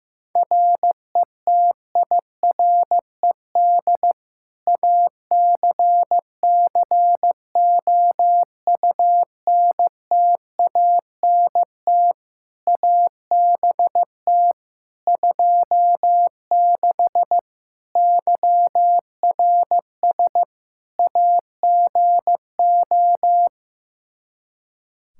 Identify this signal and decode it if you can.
Morse code, heard as a keyed tone, about 15 wpm, 700 Hz. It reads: RETIRED ACCOUNTANT ABT 26 YRS AGO
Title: Practice QSO - Exchange 3 - More Info - 15wpm